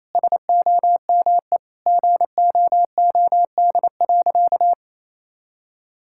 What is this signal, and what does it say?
Morse code, heard as a keyed tone, about 28 wpm, 700 Hz. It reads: SOME GOOD.
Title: Sentences from Top 1000 Words - Review Part 6 - 28wpm